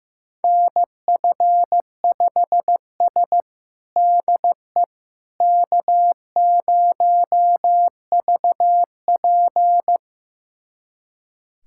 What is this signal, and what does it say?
Morse code, heard as a keyed tone, about 16 wpm, 700 Hz. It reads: NF5S DE K0VP
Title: USA Callsign DE Callsign - 4 Characters - 15wpm